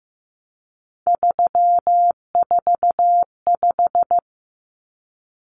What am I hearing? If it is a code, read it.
Morse code, heard as a keyed tone, about 15 wpm, 700 Hz. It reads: 345
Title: Common Cut Numbers - 3-Digit 15wpm